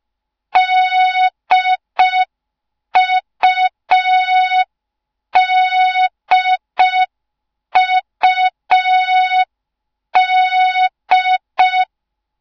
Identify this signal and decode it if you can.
Morse code, heard as a keyed tone, about 5 wpm, 3700 Hz. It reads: DUDUD
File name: Morze-HA2MN / hang / 06BDU HA2MN